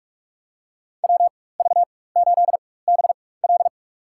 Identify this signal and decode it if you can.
Morse code, heard as a keyed tone, about 47 wpm, spaced as for 21 wpm, 700 Hz. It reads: WV8BL
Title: USA Call Signs - 4x Character Spacing - 45wpm